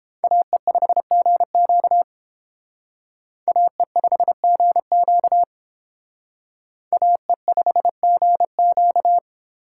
Morse code, heard as a keyed tone, about 29 wpm, 700 Hz. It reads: AE5GQ AE5GQ AE5GQ